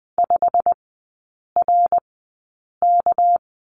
Morse code, heard as a keyed tone, about 20 wpm, spaced as for 8 wpm, 700 Hz. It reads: 5RK
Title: 198 Lesson 32e - Letter k - Three Characters 20wpm